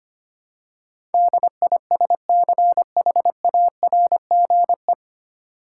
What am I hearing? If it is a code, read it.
Morse code, heard as a keyed tone, about 25 wpm, 700 Hz. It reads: DISCHARGE
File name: practice / 25wpm / 9 Letter Words - First 3 Letters Given - 25wpm